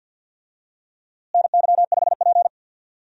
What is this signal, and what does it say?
Morse code, heard as a keyed tone, about 50 wpm, 700 Hz. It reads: NY5P
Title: USA Call Signs - 4 Characters - Repeated 2x - 50wpm